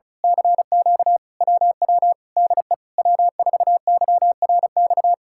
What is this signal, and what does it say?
Morse code, heard as a keyed tone, about 33 wpm, 700 Hz. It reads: CQ WW DE W4YRX
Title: Speed-Racing Common QSO Phrases Encore - 35wpm